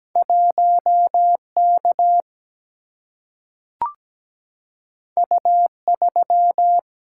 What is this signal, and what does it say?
Morse code, heard as a keyed tone, about 17 wpm, 700 Hz. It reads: 1K U3
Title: Single-Letter Single-Number 17wpm